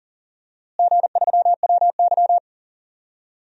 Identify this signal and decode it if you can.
Morse code, heard as a keyed tone, about 40 wpm, 700 Hz. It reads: G3WY
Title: DXCC-100 Call Signs - 4 Characters - Repeated 3x - 40wpm